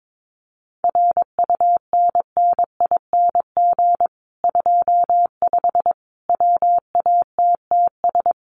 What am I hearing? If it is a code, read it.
Morse code, heard as a keyed tone, about 22 wpm, 700 Hz. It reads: RUNNING 25 WATTS
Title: Common QSO Phrases 22wpm